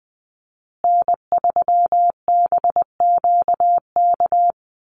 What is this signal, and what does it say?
Morse code, heard as a keyed tone, about 20 wpm, 700 Hz. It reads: N3BQK
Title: USA Call Signs Repeated 2x 20wpm